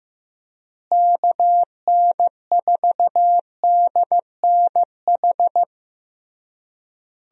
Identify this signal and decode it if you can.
Morse code, heard as a keyed tone, about 16 wpm, 700 Hz. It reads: KN4DNH
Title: USA Call Signs - 6 Characters - Repeated 3x - 15wpm